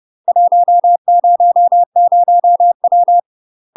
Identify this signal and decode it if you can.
Morse code, heard as a keyed tone, about 30 wpm, 700 Hz. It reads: 100W
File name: UK MASTER QSOs TRAINING/30WPM